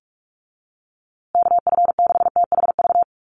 Morse code, heard as a keyed tone, about 44 wpm, 700 Hz. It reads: KF6THV
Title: USA Call Signs Repeated 2x 45wpm